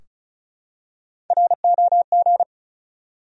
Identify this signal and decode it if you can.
Morse code, heard as a keyed tone, about 36 wpm, 700 Hz. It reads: ROG
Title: USA Common Names - Max 5 Letters - 35wpm